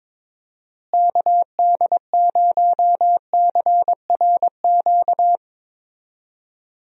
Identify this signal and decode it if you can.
Morse code, heard as a keyed tone, about 22 wpm, 700 Hz. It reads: KD0CRQ
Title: USA Call Signs - 6 Characters - Repeated 3x - 22wpm